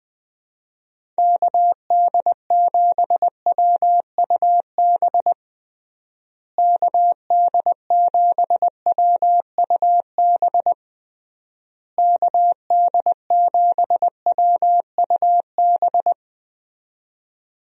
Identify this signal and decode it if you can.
Morse code, heard as a keyed tone, about 20 wpm, 700 Hz. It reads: KD7WUB KD7WUB KD7WUB
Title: USA Call Signs - 6 Characters - Repeated 3x - 20wpm